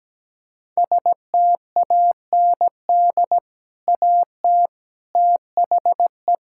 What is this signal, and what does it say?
Morse code, heard as a keyed tone, about 17 wpm, 700 Hz. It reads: STAND AT THE
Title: Sentences from Top 1600 Words 17wpm